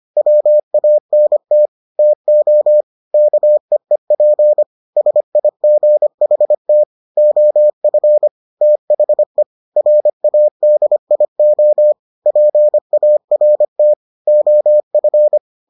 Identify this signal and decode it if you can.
Morse code, heard as a keyed tone, about 25 wpm, 600 Hz. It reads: WANT TO KEEP SIGHT OF THE RADIO PART OF